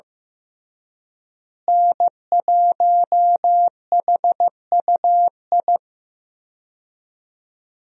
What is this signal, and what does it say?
Morse code, heard as a keyed tone, about 15 wpm, 700 Hz. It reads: N1HUI
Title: USA Call Signs - 5 Characters - Repeated 3x - 15wpm